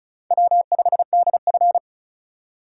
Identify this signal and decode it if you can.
Morse code, heard as a keyed tone, about 36 wpm, 700 Hz. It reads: W5DF
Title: USA Call Signs - 4 Characters - 4x Character Spacing - 35wpm